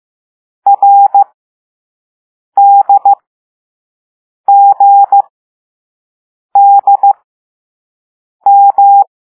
Morse code, heard as a keyed tone, about 15 wpm, spaced as for 5 wpm, 800 Hz. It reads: RDGDM